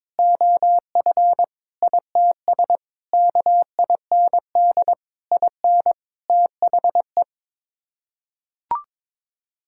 Morse code, heard as a keyed tone, about 22 wpm, 700 Hz. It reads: OF ITS KIND IN THE
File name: practice / 22wpm / Sets of 8 Words - 22wpm